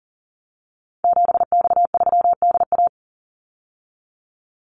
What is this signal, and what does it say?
Morse code, heard as a keyed tone, about 40 wpm, 700 Hz. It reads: 7X3DA
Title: Call Signs Repeated 3x - 40wpm